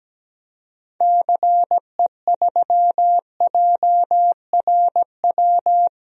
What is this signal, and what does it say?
Morse code, heard as a keyed tone, about 17 wpm, 700 Hz. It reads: CE3JRW
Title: DXCC-100 Call Signs - 6 Characters - Repeated 2x - 17wpm